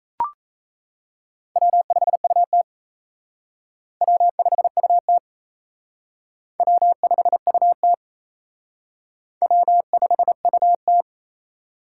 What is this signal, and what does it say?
Morse code, heard as a keyed tone, about 34 wpm, 700 Hz. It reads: W5UT W5UT W5UT W5UT
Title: Speed Racing - USA Call Signs 28wpm